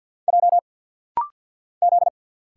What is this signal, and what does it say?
Morse code, heard as a keyed tone, about 50 wpm, 690 Hz. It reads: J Z
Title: Single Letters - Rapid-Fire 50wpm